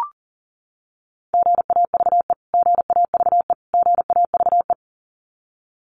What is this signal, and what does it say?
Morse code, heard as a keyed tone, about 40 wpm, 700 Hz. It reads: GAVE GAVE GAVE
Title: Top 500 Words Repeated 40wpm